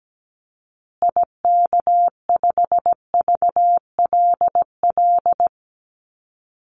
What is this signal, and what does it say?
Morse code, heard as a keyed tone, about 17 wpm, 700 Hz. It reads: IK5VLL
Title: Call Signs Repeated 2x - 17wpm